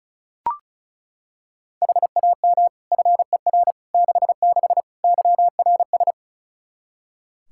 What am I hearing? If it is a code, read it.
Morse code, heard as a keyed tone, about 36 wpm, 700 Hz. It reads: HAM FER 66 YRS
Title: Practice QSO - Exchange 3 - More Info - 35wpm